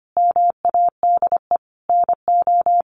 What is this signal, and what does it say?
Morse code, heard as a keyed tone, about 25 wpm, 700 Hz. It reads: MADE NO
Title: Sentences from Top 1400 Words 25wpm